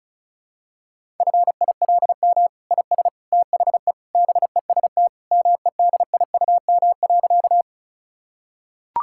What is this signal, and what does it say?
Morse code, heard as a keyed tone, about 36 wpm, 700 Hz. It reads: FILM IS THE BEST MEDIUM.
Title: Sentences from Top 1000 Words - Review Part 6 - 35wpm